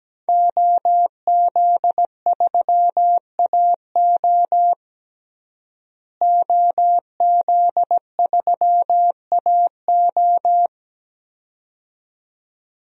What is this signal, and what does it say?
Morse code, heard as a keyed tone, about 17 wpm, 700 Hz. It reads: OZ3AO OZ3AO
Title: DXCC-100 Call Signs - 5 Characters - Repeated 2x - 17wpm